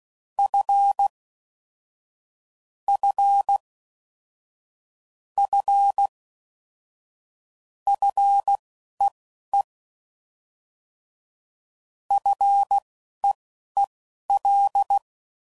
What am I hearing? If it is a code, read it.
Morse code, heard as a keyed tone, about 16 wpm, spaced as for 11 wpm, 800 Hz. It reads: F F F FEE FEEL
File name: Basic Sets / QST de WA7RSO / Combnations